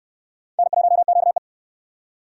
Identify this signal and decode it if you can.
Morse code, heard as a keyed tone, about 61 wpm, 700 Hz. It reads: N1QE